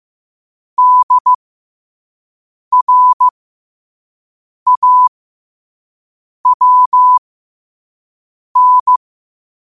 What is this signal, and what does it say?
Morse code, heard as a keyed tone, about 15 wpm, spaced as for 5 wpm, 1000 Hz. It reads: DRAWN
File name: qotdfiles / 5wpm